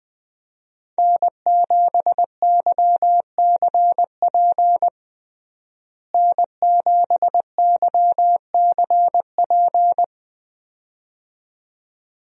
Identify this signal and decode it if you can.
Morse code, heard as a keyed tone, about 20 wpm, 700 Hz. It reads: N7YCP N7YCP
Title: USA Call Signs - 5 Characters - Repeated 2x - 20wpm